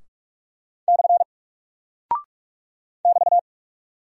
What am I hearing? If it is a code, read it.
Morse code, heard as a keyed tone, about 45 wpm, 700 Hz. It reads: / <BT>
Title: Single Character (Letter:Number:Punctuation:Prosign) Alt 45wpm